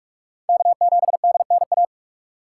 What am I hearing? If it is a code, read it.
Morse code, heard as a keyed tone, about 43 wpm, 700 Hz. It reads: K7DNA
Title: USA Call Signs - 5 Characters - 4x Character Spacing - 45wpm